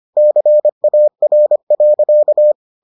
Morse code, heard as a keyed tone, about 25 wpm, 600 Hz. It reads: CAR.